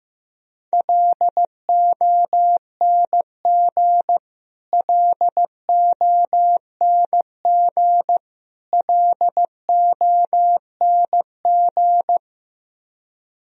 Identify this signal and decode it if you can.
Morse code, heard as a keyed tone, about 15 wpm, 700 Hz. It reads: LONG LONG LONG
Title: Top 100 Words Repeated 15wpm